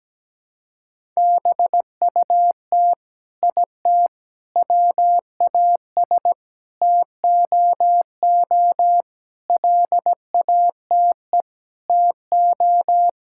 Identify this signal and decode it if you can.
Morse code, heard as a keyed tone, about 17 wpm, 700 Hz. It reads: BUT IT WAS TOO LATE TO
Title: Sentences from Top 1600 Words 17wpm